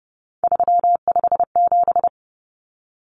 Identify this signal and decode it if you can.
Morse code, heard as a keyed tone, about 30 wpm, 700 Hz. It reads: 357
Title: Common Cut Numbers - 3-Digit 30wpm